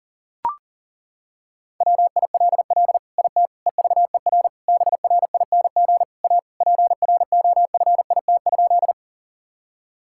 Morse code, heard as a keyed tone, about 40 wpm, 700 Hz. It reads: WILL IT EVER BRING A PROFIT?
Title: Sentences from Top 1000 Words - Review Part 5 - 40wpm